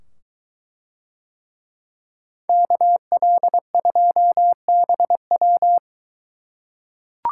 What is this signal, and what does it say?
Morse code, heard as a keyed tone, about 21 wpm, 700 Hz. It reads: KL2BW